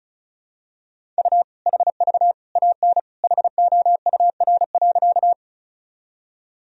Morse code, heard as a keyed tone, about 34 wpm, 700 Hz. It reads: U HV AN HOUR.
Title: Sentences from Top 400 Words and Abbreviations 35wpm